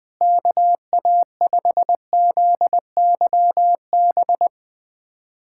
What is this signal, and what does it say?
Morse code, heard as a keyed tone, about 21 wpm, 700 Hz. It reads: KA5ZYB
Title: USA Callsign DE Callsign - 6 Characters - 20wpm